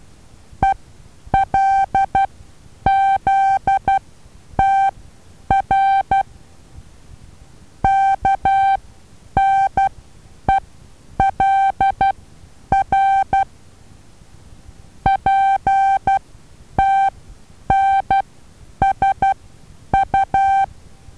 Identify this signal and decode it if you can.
Morse code, heard as a keyed tone, about 12 wpm, spaced as for 9 wpm, 780 Hz. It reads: ELZTR KNELR PTNSU
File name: Taviro HA6VA B